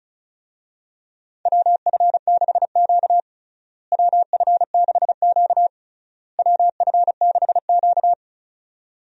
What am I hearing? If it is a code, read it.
Morse code, heard as a keyed tone, about 36 wpm, 700 Hz. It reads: WF6Q WF6Q WF6Q
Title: USA Call Signs - 4 Characters - Repeated 3x - 35wpm